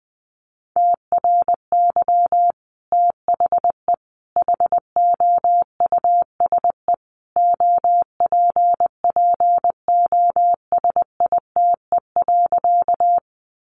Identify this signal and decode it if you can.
Morse code, heard as a keyed tone, about 20 wpm, 700 Hz. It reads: TRY THE HOUSE OPPOSITE.
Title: Sentences from Top 1300 Words 20wpm